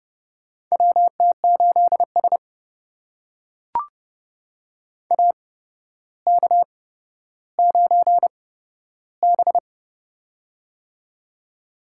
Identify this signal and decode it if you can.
Morse code, heard as a keyed tone, about 31 wpm, spaced as for 13 wpm, 700 Hz. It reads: WT8S AK9B
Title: USA Call Signs - 4 Characters - 8x Character Spacing - 30wpm